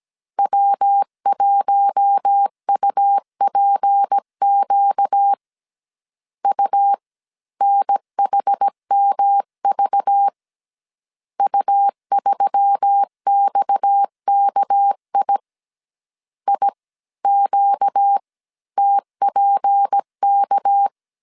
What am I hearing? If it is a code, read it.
Morse code, heard as a keyed tone, about 17 wpm, spaced as for 15 wpm, 800 Hz. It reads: W1UPQ UNHMV U3XKI IQTPK
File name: CW / DET / 1 6 1500C S A 16wpm 2635